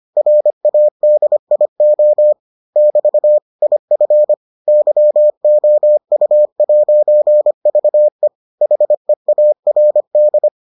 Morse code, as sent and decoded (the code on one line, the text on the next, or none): .-. .- -.. .. --- / -...- / .. ..-. / -.-- --- ..- .----. ...- . / .... .- .-. -..
RADIO <BT> IF YOU'VE HEARD